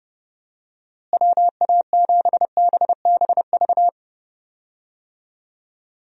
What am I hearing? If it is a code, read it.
Morse code, heard as a keyed tone, about 30 wpm, 700 Hz. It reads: WA7BBV
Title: USA Call Signs - 6 Characters - Repeated 3x - 30wpm